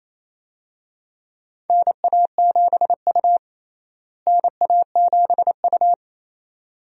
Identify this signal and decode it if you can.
Morse code, heard as a keyed tone, about 28 wpm, 700 Hz. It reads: NA7U NA7U
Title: USA Call Signs - 4 Characters - Repeated 2x - 28wpm